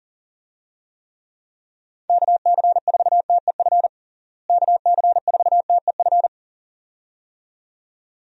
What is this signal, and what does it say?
Morse code, heard as a keyed tone, about 37 wpm, 700 Hz. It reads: KC4TEF KC4TEF
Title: USA Call Signs - 6 Characters - Repeated 2x - 40wpm